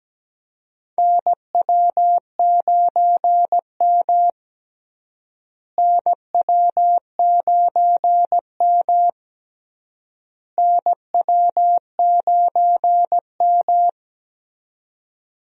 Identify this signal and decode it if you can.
Morse code, heard as a keyed tone, about 17 wpm, 700 Hz. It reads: NW9M NW9M NW9M
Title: USA Call Signs - 4 Characters - Repeated 3x - 17wpm